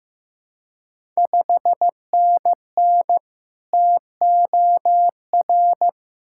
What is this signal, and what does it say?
Morse code, heard as a keyed tone, about 15 wpm, 700 Hz. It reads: 5NN TOR
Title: Ontario QSO Party Exchange 15wpm